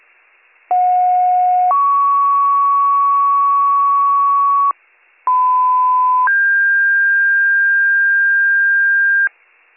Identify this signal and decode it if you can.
Two-tone page, heard as a one-second tone, then a three-second tone, about 730 then 1120 Hz, about 1010 then 1600 Hz. It is heard from 0.7 to 9.3 s.